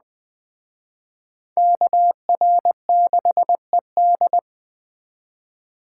Morse code, heard as a keyed tone, about 20 wpm, 700 Hz. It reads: KR6ED
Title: Speed Racing - USA 2x2 Call Signs 17wpm